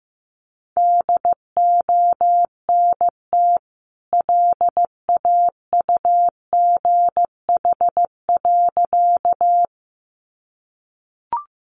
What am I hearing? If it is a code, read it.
Morse code, heard as a keyed tone, about 15 wpm, 700 Hz. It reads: DONT LAUGH.
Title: Sentences from Top 500 Words 15wpm